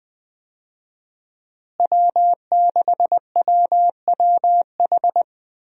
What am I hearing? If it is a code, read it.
Morse code, heard as a keyed tone, about 21 wpm, 700 Hz. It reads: W6WWH
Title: USA Call Signs - 5 Characters - Repeated 3x - 20wpm